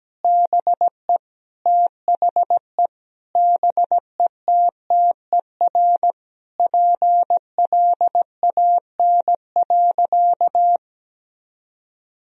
Morse code, heard as a keyed tone, about 17 wpm, 700 Hz. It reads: BE THE BETTER PLAN.
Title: Sentences from Top 1000 Words - Review Part 2 - 17wpm